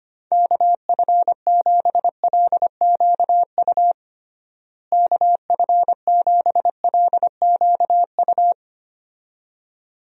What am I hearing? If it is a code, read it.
Morse code, heard as a keyed tone, about 25 wpm, 700 Hz. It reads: KF7LQU KF7LQU
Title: USA Call Signs - 6 Characters - Repeated 2x - 25wpm